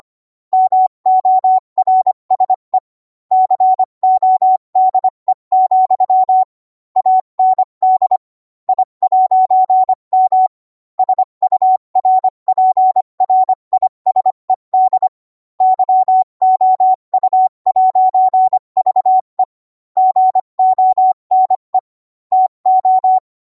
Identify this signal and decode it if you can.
Morse code, heard as a keyed tone, about 25 wpm, 750 Hz. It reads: MORSE CODE, AND I'M SURPRISED YOU'VE GONE TO